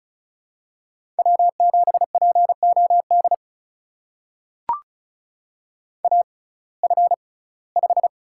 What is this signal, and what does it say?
Morse code, heard as a keyed tone, about 36 wpm, spaced as for 22 wpm, 700 Hz. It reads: W7POD AF5
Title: USA Call Signs - 5 Characters - 6x Character Spacing - 35wpm